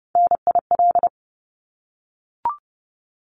Morse code, heard as a keyed tone, about 30 wpm, 700 Hz. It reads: NIL
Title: Abbreviations 30wpm